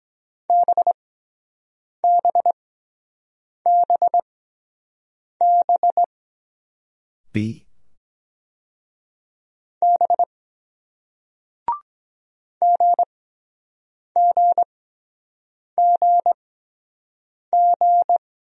Morse code, heard as a keyed tone, about 23 wpm, spaced as for 7 wpm, 700 Hz. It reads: BBBB B GGGG